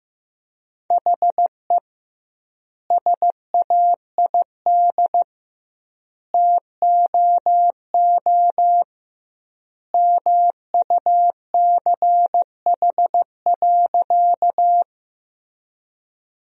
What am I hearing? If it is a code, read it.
Morse code, heard as a keyed tone, about 15 wpm, 700 Hz. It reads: HE SAID TOO MUCH.
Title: 2x Word-Spacing - Sentences from Top 200 Words - 15wpm